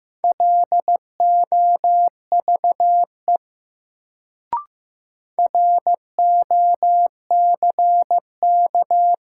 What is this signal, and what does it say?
Morse code, heard as a keyed tone, about 15 wpm, 700 Hz. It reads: LOVE ROCK
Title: Sentences from Top 400 Words - New Word Drill - 15wpm